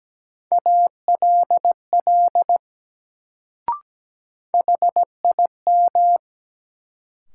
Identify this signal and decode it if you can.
Morse code, heard as a keyed tone, about 17 wpm, 700 Hz. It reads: ALL HIM
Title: Sets of 2 Words - New Word Drill - 17wpm